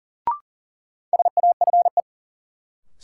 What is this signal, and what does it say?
Morse code, heard as a keyed tone, about 40 wpm, 700 Hz. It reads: SAFE
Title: Sentences from Top 900 Words - New Word Drill - 40wpm